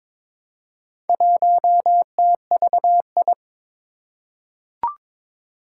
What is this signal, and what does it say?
Morse code, heard as a keyed tone, about 22 wpm, 700 Hz. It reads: 1TVI
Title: Four Challenging Letter-Number Combinations 22wpm